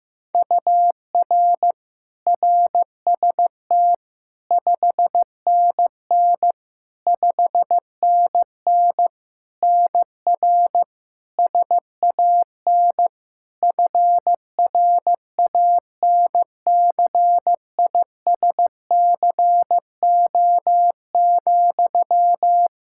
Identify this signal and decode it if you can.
Morse code, heard as a keyed tone, about 15 wpm, 700 Hz. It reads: UR RST 5NN 5NN NR SAN FRANCISCO,
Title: Practice QSO - Exchange 1 - Beginning - 15wpm